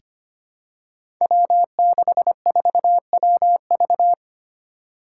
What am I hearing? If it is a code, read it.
Morse code, heard as a keyed tone, about 25 wpm, 700 Hz. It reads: W64WV
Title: Five Challenging Letter-Number Combinations 25wpm